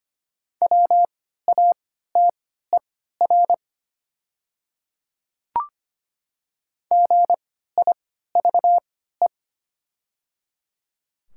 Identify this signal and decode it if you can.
Morse code, heard as a keyed tone, about 25 wpm, spaced as for 14 wpm, 700 Hz. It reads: WATER GIVE
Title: 3x Character Spacing - Top 200 Words - 25wpm